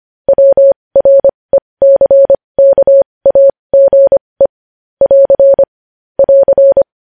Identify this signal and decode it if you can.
Morse code, heard as a keyed tone, about 25 wpm, spaced as for 20 wpm, 550 Hz. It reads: WRECKAGE <AR> <AR>
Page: News headlines in Morse code at 20 wpm